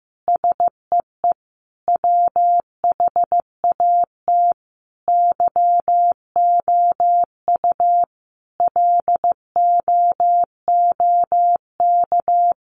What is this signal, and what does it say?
Morse code, heard as a keyed tone, about 15 wpm, 700 Hz. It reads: SEE WHAT YOU LOOK
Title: Sentences from Top 1300 Words 15wpm